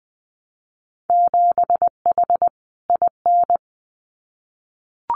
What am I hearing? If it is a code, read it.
Morse code, heard as a keyed tone, about 19 wpm, 700 Hz. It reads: 7H IN
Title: Winter Field Day Exchanges Encore 20wpm